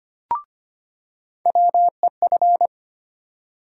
Morse code, heard as a keyed tone, about 25 wpm, 700 Hz. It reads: WEF
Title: USA Call Sign Postfixes - 3 Letters 25wpm